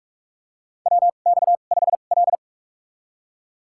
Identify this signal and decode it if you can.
Morse code, heard as a keyed tone, about 46 wpm, spaced as for 33 wpm, 700 Hz. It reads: WX5L
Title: USA Call Signs - 4 Characters - 2x Character Spacing - 45wpm